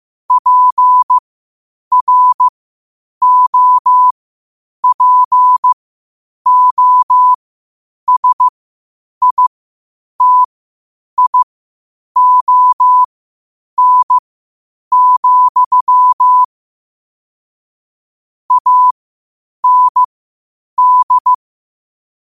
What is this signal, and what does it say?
Morse code, heard as a keyed tone, about 15 wpm, spaced as for 8 wpm, 1000 Hz. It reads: PROPOSITION, AND
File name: qotdfiles / 8wpm